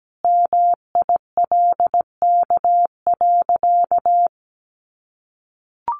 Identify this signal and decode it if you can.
Morse code, heard as a keyed tone, about 17 wpm, 700 Hz. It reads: MILK.
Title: Sentences from Top 1500 Words 17wpm